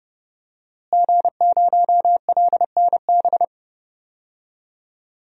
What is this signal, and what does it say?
Morse code, heard as a keyed tone, about 31 wpm, 700 Hz. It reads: G0LNB
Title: DXCC-100 Call Signs - 5 Characters - Repeated 3x - 30wpm